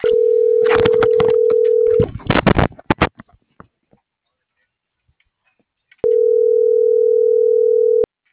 Ringback tone (North America), 440 + 480 Hz.